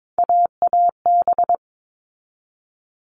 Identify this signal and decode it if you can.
Morse code, heard as a keyed tone, about 22 wpm, 700 Hz. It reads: AAB
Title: All Cut Numbers - 3-Digit 22wpm